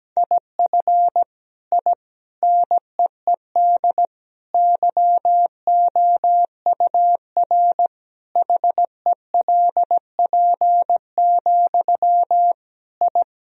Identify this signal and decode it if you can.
Morse code, heard as a keyed tone, about 17 wpm, 700 Hz. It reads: IF I NEED YOUR HELP, I
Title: Sentences from Top 1000 Words - Review Part 1 - 17wpm